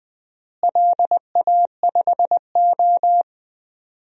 Morse code, heard as a keyed tone, about 20 wpm, 700 Hz. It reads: LA5O
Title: DXCC-100 Call Signs - 4 Characters - Repeated 3x - 20wpm